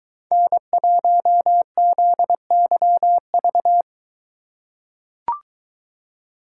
Morse code, heard as a keyed tone, about 21 wpm, 700 Hz. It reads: N1ZYV